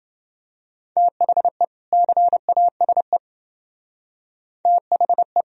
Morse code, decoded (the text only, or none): THE CASE THE